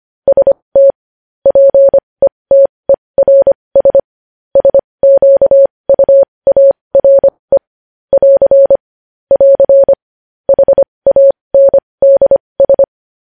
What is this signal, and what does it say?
Morse code, heard as a keyed tone, about 25 wpm, spaced as for 20 wpm, 550 Hz. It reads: ST PETERS SQUARE <AR> <AR> HANDS
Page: news headlines in morse code at 20 wpm